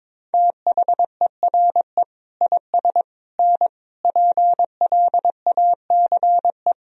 Morse code, heard as a keyed tone, about 22 wpm, 700 Hz. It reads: THERE IS N PLACE
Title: Sentences from Top 200 Words and Abbreviations 22wpm